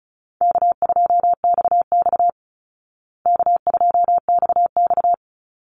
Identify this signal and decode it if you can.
Morse code, heard as a keyed tone, about 34 wpm, 700 Hz. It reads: K2XX K2XX
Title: Call Signs Repeated 3x - 35wpm